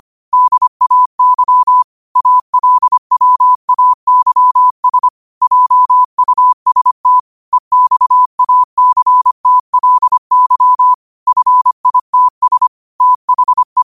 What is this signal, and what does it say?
Morse code, heard as a keyed tone, about 25 wpm, 1000 Hz. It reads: DAY ALWAYS JUST EXACTLY FITS THE
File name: qotdfiles / 25wpm